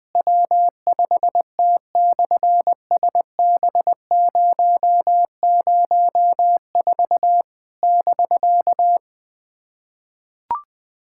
Morse code, heard as a keyed tone, about 20 wpm, 700 Hz. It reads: W5T/SB004 <BK>
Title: SOTA QSO Practice 20wpm